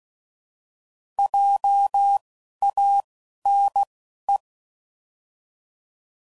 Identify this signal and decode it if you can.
Morse code, heard as a keyed tone, about 16 wpm, spaced as for 12 wpm, 800 Hz. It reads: JANE